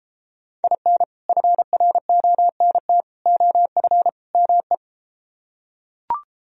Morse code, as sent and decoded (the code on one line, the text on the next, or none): .. -. / ..-. .-. --- -. - / --- ..-. / -- .
IN FRONT OF ME